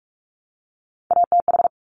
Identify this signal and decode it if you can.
Morse code, heard as a keyed tone, about 46 wpm, 700 Hz. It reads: ATH
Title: Top Three Letter Combinations in Words 45wpm